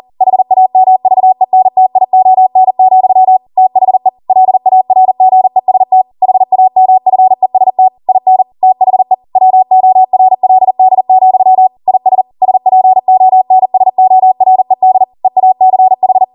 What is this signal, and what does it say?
Morse code, heard as a keyed tone, about 40 wpm, 750 Hz. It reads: HAMVENTION, THE LARGEST HAMFEST IN THE WORLD, IS SPONSORED EACH